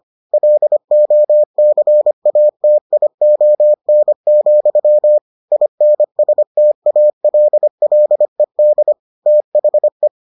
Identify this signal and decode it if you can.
Morse code, heard as a keyed tone, about 25 wpm, 600 Hz. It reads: LOCATION, INSTALLED THE